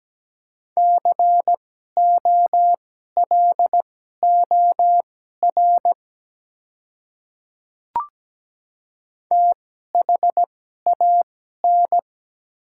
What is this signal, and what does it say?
Morse code, heard as a keyed tone, about 17 wpm, spaced as for 12 wpm, 700 Hz. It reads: COLOR THAN
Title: 2x Character Spacing - Top 300 Words - 17wpm